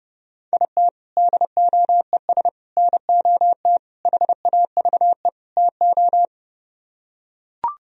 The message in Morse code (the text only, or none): IT DOES NOT HAVE TO